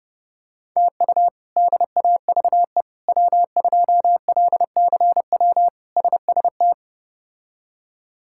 Morse code, heard as a keyed tone, about 30 wpm, 700 Hz. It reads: TU DAVE W2LCW SST
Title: K1USN SST QSO Practice - 30wpm